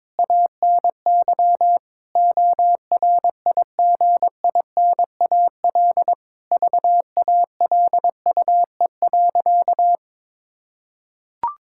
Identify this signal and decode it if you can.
Morse code, heard as a keyed tone, about 21 wpm, 700 Hz. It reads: ANY ORIGINAL VALUE.
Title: Sentences from Top 1000 Words - Review Part 6 - 22wpm